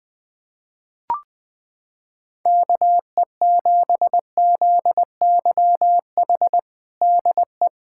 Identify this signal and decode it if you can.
Morse code, heard as a keyed tone, about 20 wpm, 700 Hz. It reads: KE7ZYH DE
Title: USA Callsign DE Callsign - 6 Characters - 20wpm